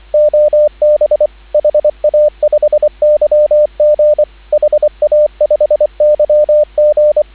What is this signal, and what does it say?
Morse code, heard as a keyed tone, about 24 wpm, 600 Hz, with radio noise. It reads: OB HA5YG HA5YG
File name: CW / 2017 HA-OB HA5YG